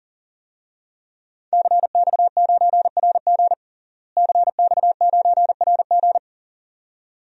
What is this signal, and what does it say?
Morse code, heard as a keyed tone, about 40 wpm, 700 Hz. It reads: CX9RG CX9RG
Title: DXCC-100 Call Signs - 5 Characters - Repeated 2x - 40wpm